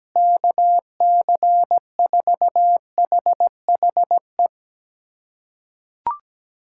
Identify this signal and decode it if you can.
Morse code, heard as a keyed tone, about 17 wpm, 700 Hz. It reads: KC4HHE
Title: USA Callsign DE Callsign 17wpm